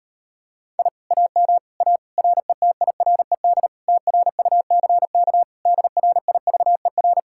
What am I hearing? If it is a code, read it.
Morse code, heard as a keyed tone, about 33 wpm, 700 Hz. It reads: I AM A RETIRED TRUCK DRIVER